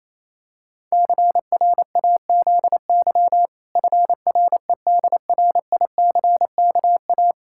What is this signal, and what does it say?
Morse code, heard as a keyed tone, about 28 wpm, 700 Hz. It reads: CRAZY FREDRICKA
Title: Pangram Challenge v2 - 28wpm